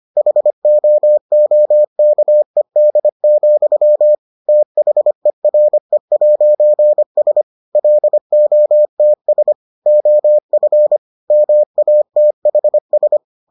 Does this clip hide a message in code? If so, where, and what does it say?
Yes: Morse code from 0.2 to 13.2 s, heard as a keyed tone, about 25 wpm, 600 Hz. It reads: HOOKED, THERE'S LOTS OF MATHS